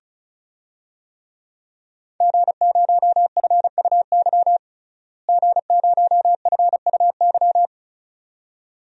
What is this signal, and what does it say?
Morse code, heard as a keyed tone, about 34 wpm, 700 Hz. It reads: G0FUY G0FUY
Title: DXCC-100 Call Signs - 5 Characters - Repeated 2x - 35wpm